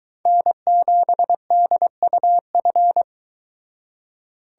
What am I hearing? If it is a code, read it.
Morse code, heard as a keyed tone, about 23 wpm, 700 Hz. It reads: N7DUF